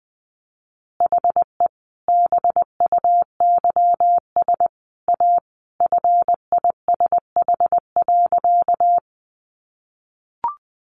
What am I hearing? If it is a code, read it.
Morse code, heard as a keyed tone, about 20 wpm, 700 Hz. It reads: HE BUYS A FISH.
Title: Additional Easy Phrases 20wpm